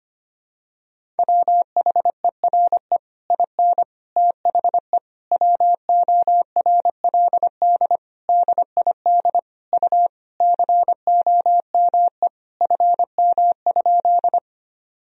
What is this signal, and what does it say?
Morse code, heard as a keyed tone, about 25 wpm, 700 Hz. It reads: WHERE IN THE WORLD DID U COME FM?
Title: Sentences from Top 200 Words and Abbreviations 25wpm